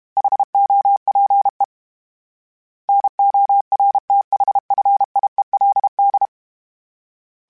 Morse code, heard as a keyed tone, about 32 wpm, 800 Hz. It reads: HOPE NORTHFIELD
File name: fromSMs / FromSMs mp3s / Cities@30